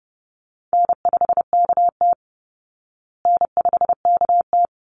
Morse code, heard as a keyed tone, about 28 wpm, 700 Hz. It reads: N5KT N5KT
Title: Easy Call Signs Repeated 3x - 30wpm